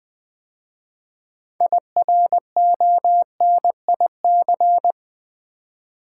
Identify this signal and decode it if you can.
Morse code, heard as a keyed tone, about 20 wpm, 700 Hz. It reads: IRONIC
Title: 6 Letter Words - First 3 Letters Given - 20wpm